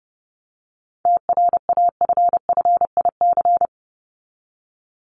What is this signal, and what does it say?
Morse code, heard as a keyed tone, about 30 wpm, 700 Hz. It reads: TRAFFIC
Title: Top 1251-1500 Words 30wpm